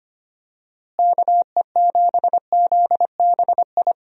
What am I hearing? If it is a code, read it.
Morse code, heard as a keyed tone, about 25 wpm, 700 Hz. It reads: KE7ZBI
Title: USA Call Signs - 6 Characters - Repeated 2x - 25wpm